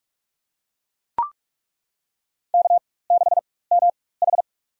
Morse code, heard as a keyed tone, about 42 wpm, spaced as for 20 wpm, 700 Hz. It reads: K6MH
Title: USA Call Signs - 4 Characters - 4x Character Spacing - 45wpm